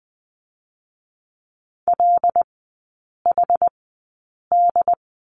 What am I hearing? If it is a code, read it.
Morse code, heard as a keyed tone, about 20 wpm, spaced as for 8 wpm, 700 Hz. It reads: LHD
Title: 074 Lesson 13e - Letter L - Three Characters 20wpm